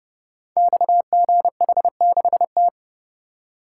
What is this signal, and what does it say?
Morse code, heard as a keyed tone, about 31 wpm, 700 Hz. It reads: XGH6T
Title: Code Groups - 5 Letter:Numbers - 30wpm